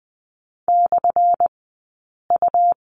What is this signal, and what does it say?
Morse code, heard as a keyed tone, about 20 wpm, spaced as for 8 wpm, 700 Hz. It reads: /U
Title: 183 Lesson 30d - Letter b - Two Characters 20wpm